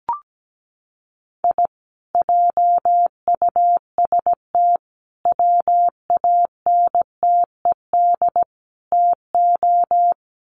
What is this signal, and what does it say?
Morse code, heard as a keyed tone, about 17 wpm, 700 Hz. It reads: I JUST WANTED TO
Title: Sets of 4 Words 17wpm